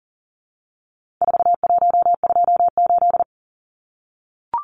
Morse code, heard as a keyed tone, about 40 wpm, 700 Hz. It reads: 4128
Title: Numbers - 4-Digit 40wpm